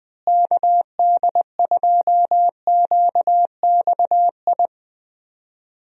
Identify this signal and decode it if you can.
Morse code, heard as a keyed tone, about 20 wpm, 700 Hz. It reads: KD2QXI
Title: USA Call Signs - 6 Characters - Repeated 3x - 20wpm